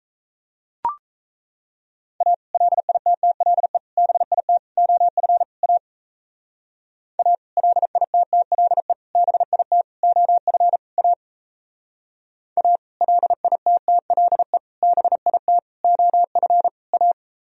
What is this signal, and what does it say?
Morse code, heard as a keyed tone, about 35 wpm, 700 Hz. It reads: A LITTLE BIT OF A A LITTLE BIT OF A A LITTLE BIT OF A
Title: Speed-Racing - Sets of 5 Words 28wpm